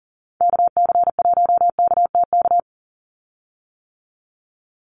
Morse code, heard as a keyed tone, about 40 wpm, 700 Hz. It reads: KC1KTK
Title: USA Call Signs 40wpm